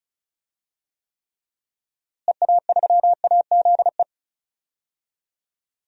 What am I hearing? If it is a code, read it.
Morse code, heard as a keyed tone, about 36 wpm, 700 Hz. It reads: EA3AZE
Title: DXCC-100 Call Signs - 6 Characters - Repeated 3x - 35wpm